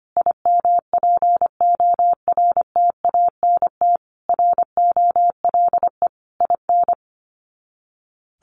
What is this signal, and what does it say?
Morse code, heard as a keyed tone, about 25 wpm, 700 Hz. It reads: IMPORTANT ROLE IN
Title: Sets of 5 Words Encore2 25wpm